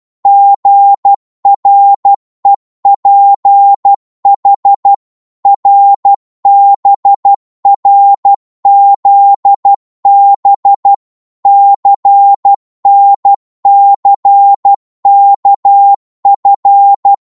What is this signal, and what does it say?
Morse code, heard as a keyed tone, about 12 wpm, 800 Hz. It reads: GREPH RBRZB CNCKF